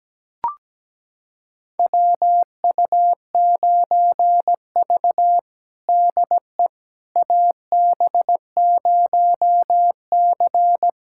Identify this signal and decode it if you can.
Morse code, heard as a keyed tone, about 17 wpm, 700 Hz. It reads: WU9V DE AB0C
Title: USA Callsign DE Callsign - 4 Characters - 17wpm